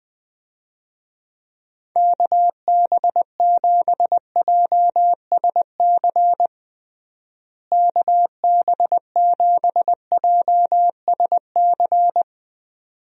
Morse code, heard as a keyed tone, about 20 wpm, 700 Hz. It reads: KB7JSC KB7JSC
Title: USA Call Signs - 6 Characters - Repeated 2x - 20wpm